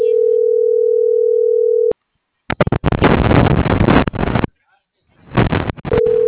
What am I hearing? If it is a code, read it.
Ringback tone (North America), 440 + 480 Hz.